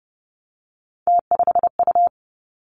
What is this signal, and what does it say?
Morse code, heard as a keyed tone, about 32 wpm, 700 Hz. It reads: T5U
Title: Three Challenging Letter-Number Combinations 30wpm